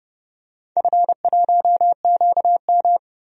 Morse code, heard as a keyed tone, about 31 wpm, 700 Hz. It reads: F1QM
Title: DXCC-100 Call Signs - 4 Characters - Repeated 2x - 30wpm